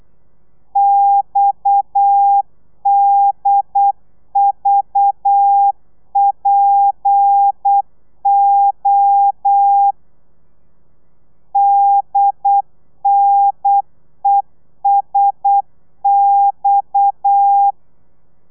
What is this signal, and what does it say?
Morse code, heard as a keyed tone, about 8 wpm, 800 Hz. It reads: XDVPO DNESX